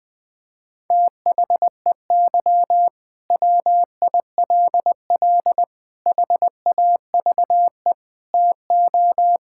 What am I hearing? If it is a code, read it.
Morse code, heard as a keyed tone, about 20 wpm, 700 Hz. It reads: THEY WILL HAVE TO
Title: Sentences from Top 1000 Words - Review Part 6 - 20wpm